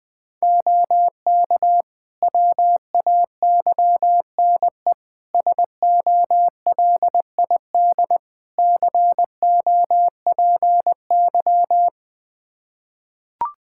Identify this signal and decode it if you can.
Morse code, heard as a keyed tone, about 21 wpm, 700 Hz. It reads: OK WAYNE SOLID COPY
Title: Practice QSO - Exchange 3 - More Info - 20wpm